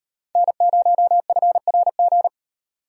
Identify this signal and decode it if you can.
Morse code, heard as a keyed tone, about 36 wpm, 700 Hz. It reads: N0FRG